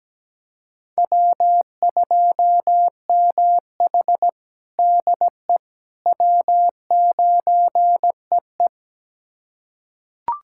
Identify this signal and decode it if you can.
Morse code, heard as a keyed tone, about 17 wpm, 700 Hz. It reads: W2MH DE W9EE
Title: USA Callsign DE Callsign - 4 Characters - 17wpm